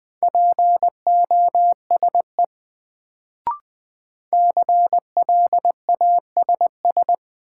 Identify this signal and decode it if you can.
Morse code, heard as a keyed tone, about 20 wpm, 700 Hz. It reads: POSE CLASS
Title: Sentences from Top 1000 Words - Review Part 2 - New Word Drill - 20wpm